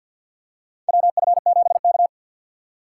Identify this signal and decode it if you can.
Morse code, heard as a keyed tone, about 53 wpm, 700 Hz. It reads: WF7K
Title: USA Call Signs - 4x Character Spacing - 50wpm